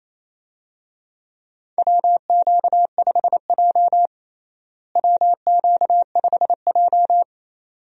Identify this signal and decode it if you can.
Morse code, heard as a keyed tone, about 28 wpm, 700 Hz. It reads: WQ5J WQ5J
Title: USA Call Signs - 4 Characters - Repeated 2x - 28wpm